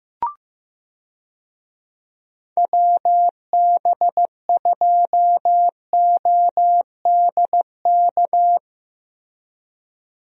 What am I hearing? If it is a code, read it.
Morse code, heard as a keyed tone, about 15 wpm, 700 Hz. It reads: WB2ODK
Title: USA Call Signs - 6 Characters - Repeated 2x - 15wpm